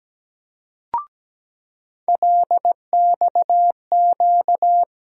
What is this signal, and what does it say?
Morse code, heard as a keyed tone, about 17 wpm, 700 Hz. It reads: LXQ
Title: USA Call Sign Postfixes - 3 Letters 17wpm